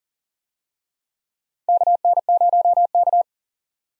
Morse code, heard as a keyed tone, about 39 wpm, 700 Hz. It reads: KN0K
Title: USA Call Signs - 4 Characters - 40wpm